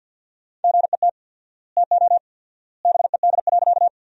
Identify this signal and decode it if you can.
Morse code, heard as a keyed tone, about 47 wpm, 700 Hz. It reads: GET TO BED.
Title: 4x Word-Spacing - Sentences from Top 500 Words - 50wpm